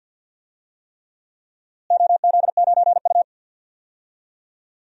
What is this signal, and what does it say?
Morse code, heard as a keyed tone, about 52 wpm, 700 Hz. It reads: OZ9U
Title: DXCC-100 Call Signs - 4 Characters - Repeated 3x - 50wpm